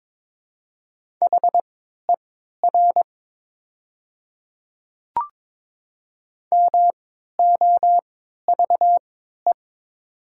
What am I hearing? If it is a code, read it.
Morse code, heard as a keyed tone, about 22 wpm, spaced as for 12 wpm, 700 Hz. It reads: HER MOVE
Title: 3x Character Spacing - Top 200 Words - 22wpm